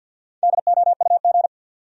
Morse code, heard as a keyed tone, about 48 wpm, 700 Hz. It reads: DOUG
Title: USA Common Names - Max 5 Letters - 50wpm